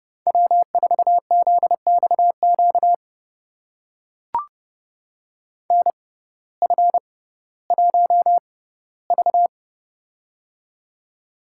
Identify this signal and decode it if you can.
Morse code, heard as a keyed tone, about 30 wpm, spaced as for 17 wpm, 700 Hz. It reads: W4ZXQ NF1V
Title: USA Call Signs - 6x Character Spacing - 30wpm